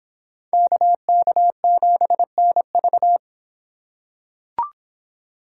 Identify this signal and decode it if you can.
Morse code, heard as a keyed tone, about 25 wpm, 700 Hz. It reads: KK7NV